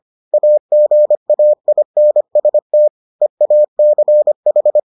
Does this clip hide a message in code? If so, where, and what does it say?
Yes: Morse code from 0.3 to 4.8 s, heard as a keyed tone, about 25 wpm, 600 Hz. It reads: AGAINST EACH